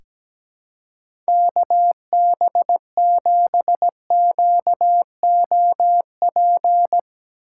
Morse code, heard as a keyed tone, about 17 wpm, 700 Hz. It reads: KB7QOP
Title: K1USN SST QSO Practice - 17wpm